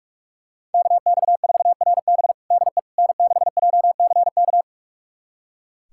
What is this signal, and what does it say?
Morse code, heard as a keyed tone, about 46 wpm, 700 Hz. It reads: KX4RB DE N6JCK
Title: USA Callsign DE Callsign - 5 Characters - 45wpm